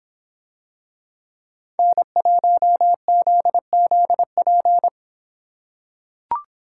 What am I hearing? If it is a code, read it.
Morse code, heard as a keyed tone, about 26 wpm, 700 Hz. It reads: N1ZZP